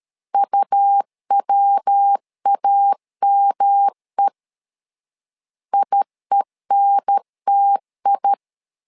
Morse code, heard as a keyed tone, about 13 wpm, 800 Hz. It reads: UWAME IENTI
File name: LEZIONI / 3 MWRO / 3 MW 12wpm 1633